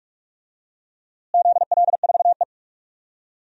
Morse code, heard as a keyed tone, about 46 wpm, 700 Hz. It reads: ZL4E
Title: DXCC-100 Call Signs - 4 Characters - Repeated 2x - 45wpm